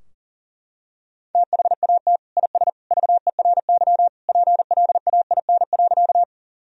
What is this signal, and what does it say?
Morse code, heard as a keyed tone, about 40 wpm, 700 Hz. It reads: THAT IS VERY PLAIN.
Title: Sentences from Top 1000 Words - Review Part 2 - 40wpm